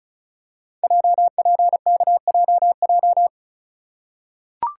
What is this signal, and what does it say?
Morse code, heard as a keyed tone, about 34 wpm, 700 Hz. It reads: JPKJJ
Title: Code Groups - 5 Letter:Numbers - 35wpm